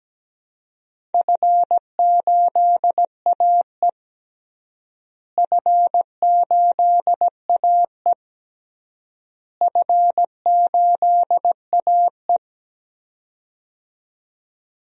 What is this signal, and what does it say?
Morse code, heard as a keyed tone, about 17 wpm, 700 Hz. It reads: F8AE F8AE F8AE
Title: DXCC-100 Call Signs - 4 Characters - Repeated 3x - 17wpm